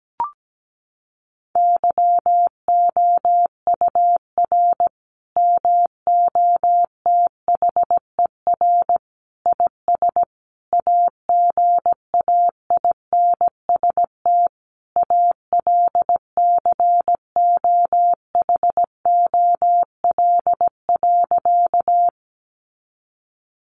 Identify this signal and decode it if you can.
Morse code, heard as a keyed tone, about 17 wpm, 700 Hz. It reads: YOUR MOTHER IS AGAINST ALCOHOL.
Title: Sentences from Top 1400 Words 17wpm